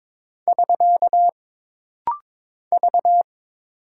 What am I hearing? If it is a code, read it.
Morse code, heard as a keyed tone, about 22 wpm, 700 Hz. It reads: <SK> V
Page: This rapid-fire timing is geared to push your ICR (Instant Character Recognition) skills.